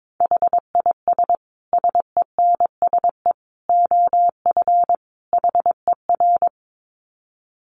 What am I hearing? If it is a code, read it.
Morse code, heard as a keyed tone, about 22 wpm, 700 Hz. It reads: HIS SENSE OF HER
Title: Dit Challenge 22wpm